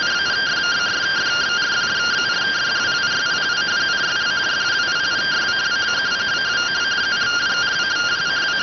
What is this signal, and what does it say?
Two-tone teleprinter signal, about 170 Hz shift.